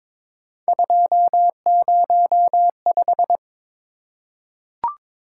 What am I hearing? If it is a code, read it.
Morse code, heard as a keyed tone, about 22 wpm, 700 Hz. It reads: 205
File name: practice / 22wpm / Three Letter-Number Combinations - 22wpm